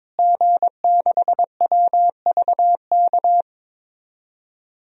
Morse code, heard as a keyed tone, about 22 wpm, 700 Hz. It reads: G6WVK
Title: DXCC-100 Call Signs - 5 Characters - Repeated 3x - 22wpm